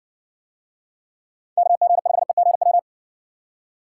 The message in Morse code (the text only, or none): KG5EGW